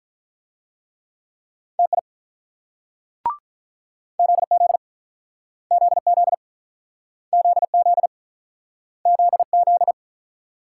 Morse code, heard as a keyed tone, about 48 wpm, 700 Hz. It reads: TI ZZ ZZ ZZ ZZ